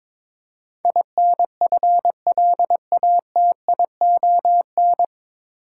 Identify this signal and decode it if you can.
Morse code, heard as a keyed tone, about 22 wpm, 700 Hz. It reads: INFLATION
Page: Morse Code Ninja practice 22wpm